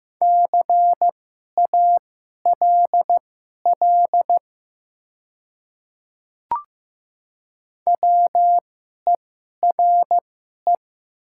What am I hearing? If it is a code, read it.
Morse code, heard as a keyed tone, about 15 wpm, spaced as for 11 wpm, 700 Hz. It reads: CALL WERE
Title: 2x Character Spacing - Top 500 Words - 15wpm